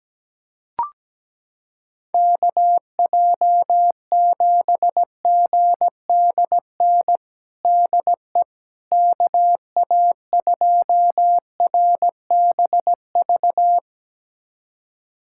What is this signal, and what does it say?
Morse code, heard as a keyed tone, about 17 wpm, 700 Hz. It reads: KJ7GDN DE KA2RBV
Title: USA Callsign DE Callsign - 6 Characters - 17wpm